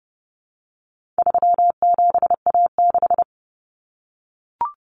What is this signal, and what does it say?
Morse code, heard as a keyed tone, about 30 wpm, 700 Hz. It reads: 37A6
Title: Common Cut Numbers - 4-Digit 30wpm